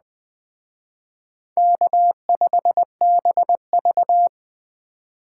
Morse code, heard as a keyed tone, about 19 wpm, 700 Hz. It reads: K5BV